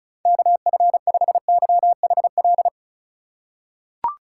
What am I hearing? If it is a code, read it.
Morse code, heard as a keyed tone, about 36 wpm, 700 Hz. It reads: KF5YHL
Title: USA Call Signs - 6 Characters - 3x Character Spacing - 35wpm